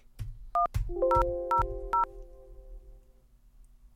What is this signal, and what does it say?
DTMF, heard as telephone touch tones, tones of about 130 ms. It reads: 1000